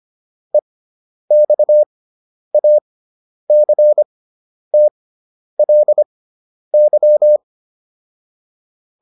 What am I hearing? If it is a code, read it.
Morse code, heard as a keyed tone, about 25 wpm, spaced as for 10 wpm, 600 Hz. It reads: EXACTLY